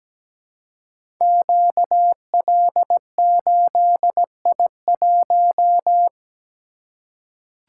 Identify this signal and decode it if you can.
Morse code, heard as a keyed tone, about 17 wpm, 700 Hz. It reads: QL8I1
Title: Code Groups - 5 Letter:Numbers - 17wpm